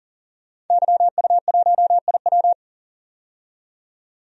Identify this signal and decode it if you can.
Morse code, heard as a keyed tone, about 40 wpm, 700 Hz. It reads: YU1IW
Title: DXCC-100 Call Signs - 5 Characters - 40wpm